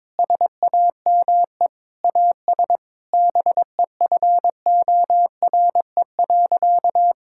Morse code, heard as a keyed tone, about 22 wpm, 700 Hz. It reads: SAME AS BEFORE.
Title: Sentences from Top 1000 Words - Review Part 6 - 22wpm